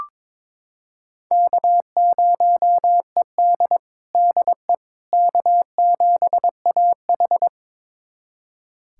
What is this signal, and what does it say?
Morse code, heard as a keyed tone, about 23 wpm, 700 Hz. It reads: K0ED DE K7AH
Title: USA Callsign DE Callsign - 4 Characters - 22wpm